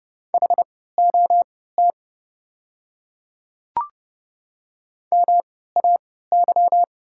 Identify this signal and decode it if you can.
Morse code, heard as a keyed tone, about 30 wpm, spaced as for 17 wpm, 700 Hz. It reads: HOT MAY
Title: 3x Character Spacing - Top 100 Words - 30wpm